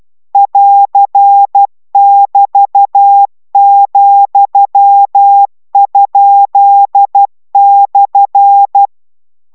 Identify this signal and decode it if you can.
Morse code, heard as a keyed tone, about 12 wpm, 800 Hz. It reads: <AR><BT>,?/